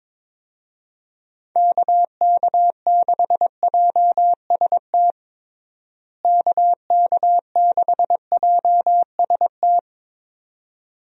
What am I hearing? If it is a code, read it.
Morse code, heard as a keyed tone, about 22 wpm, 700 Hz. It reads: KK6JST KK6JST
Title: USA Call Signs - 6 Characters - Repeated 2x - 22wpm